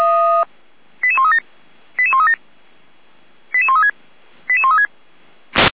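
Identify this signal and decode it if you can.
Five-tone selective call (ZVEI), heard as quick runs of five tones, tones of about 70 ms. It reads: 88137 88137 88136 88136
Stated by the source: ZVEI five-tone selective call received on an SDR.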